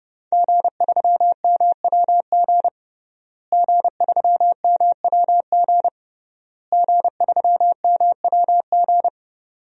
Morse code, heard as a keyed tone, about 30 wpm, 700 Hz. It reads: G3MWG G3MWG G3MWG
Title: DXCC-100 Call Signs - 5 Characters - Repeated 3x - 30wpm